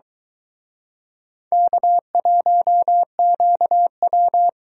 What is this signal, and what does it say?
Morse code, heard as a keyed tone, about 24 wpm, 700 Hz. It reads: K1QW